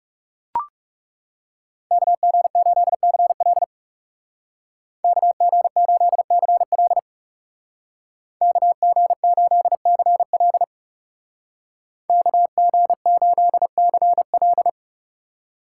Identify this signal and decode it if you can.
Morse code, heard as a keyed tone, about 36 wpm, 700 Hz. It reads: KG8CL KG8CL KG8CL KG8CL
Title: Speed Racing - USA 2x2 Call Signs 30wpm